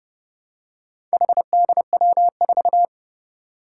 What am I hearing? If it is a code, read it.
Morse code, heard as a keyed tone, about 29 wpm, 700 Hz. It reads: HDW4
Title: Four Challenging Letter-Number Combinations 30wpm